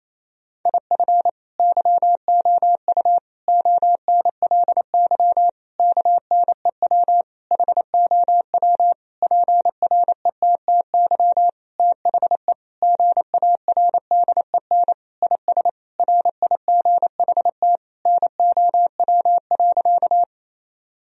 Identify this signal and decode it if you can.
Morse code, heard as a keyed tone, about 28 wpm, 700 Hz. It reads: IF YOU ONLY KNEW HOW PRETTY THE GARDEN IS RIGHT NOW.
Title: Sentences from Top 1000 Words - Review Part 4 - 28wpm